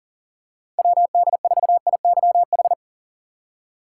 Morse code, heard as a keyed tone, about 40 wpm, 700 Hz. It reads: WD4IYH
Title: USA Call Signs - 2x Character Spacing - 40wpm